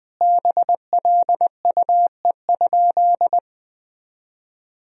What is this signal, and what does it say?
Morse code, heard as a keyed tone, about 20 wpm, 700 Hz. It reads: BLUE?
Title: Sentences from Top 1000 Words - Review Part 3 - 20wpm